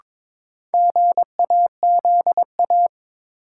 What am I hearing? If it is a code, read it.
Morse code, heard as a keyed tone, about 22 wpm, 700 Hz. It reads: GAZA
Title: Sets of 5 Words - New Word Drill - 22wpm